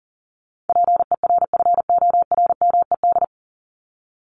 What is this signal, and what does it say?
Morse code, heard as a keyed tone, about 40 wpm, 700 Hz. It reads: PERFORMED
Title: ED Suffix - Top 100 Words with Base Words 40wpm